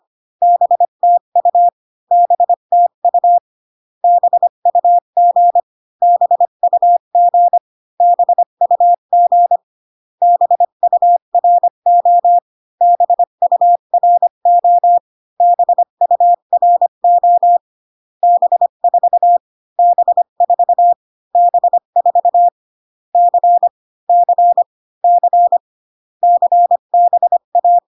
Morse code, heard as a keyed tone, about 25 wpm, 700 Hz. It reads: BTU BTU BUG BUG BUG BURO BURO BURO B4 B4 B4 C C C CBA